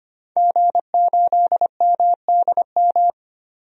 Morse code, heard as a keyed tone, about 25 wpm, 700 Hz. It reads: G8MDM
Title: DXCC-100 Call Signs - 5 Characters - 25wpm